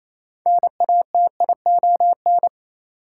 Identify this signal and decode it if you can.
Morse code, heard as a keyed tone, about 28 wpm, 700 Hz. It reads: NATION
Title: Sentences from Top 900 Words - New Word Drill - 28wpm